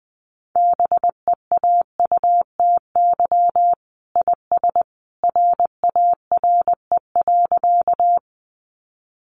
Morse code, heard as a keyed tone, about 20 wpm, 700 Hz. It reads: BEAUTY IS RARE.
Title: Sentences from Top 1100 Words 20wpm